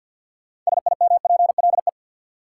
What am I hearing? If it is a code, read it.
Morse code, heard as a keyed tone, about 53 wpm, 700 Hz. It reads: SIMPLE
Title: Sentences from Top 400 Words - New Word Drill - 50wpm